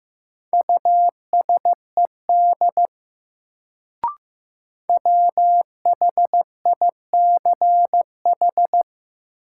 Morse code, heard as a keyed tone, about 15 wpm, 700 Hz. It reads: USED WHICH
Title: Sets of 2 Words - New Word Drill - 15wpm